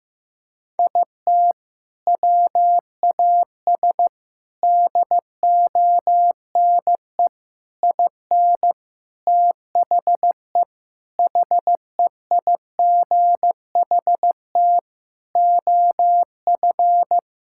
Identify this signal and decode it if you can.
Morse code, heard as a keyed tone, about 16 wpm, 700 Hz. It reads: IT WAS DONE IN THE HEIGHT OF
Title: Sentences from Top 1600 Words 15wpm